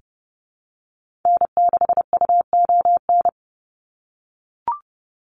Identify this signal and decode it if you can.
Morse code, heard as a keyed tone, about 30 wpm, 700 Hz. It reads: N6UON
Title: USA Call Signs 30wpm